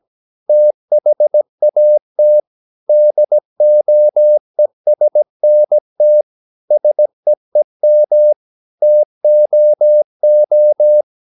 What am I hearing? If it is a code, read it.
Morse code, heard as a keyed tone, about 17 wpm, 600 Hz. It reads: THAT DOESNT SEEM TOO
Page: Romanen Never af Ken Follet på engelsk.